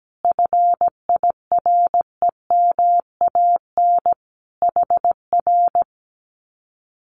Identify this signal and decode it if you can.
Morse code, heard as a keyed tone, about 17 wpm, 700 Hz. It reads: FIREMAN HR
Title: Common QSO Phrases Encore 17wpm